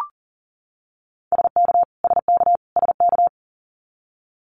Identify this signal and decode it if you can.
Morse code, heard as a keyed tone, about 40 wpm, 700 Hz. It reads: SK SK SK
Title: Canadian Provinces and Territories - Repeated 3x - 40wpm